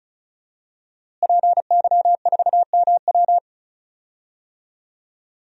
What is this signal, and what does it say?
Morse code, heard as a keyed tone, about 36 wpm, 700 Hz. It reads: PY4MW
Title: DXCC-100 Call Signs - 5 Characters - Repeated 3x - 35wpm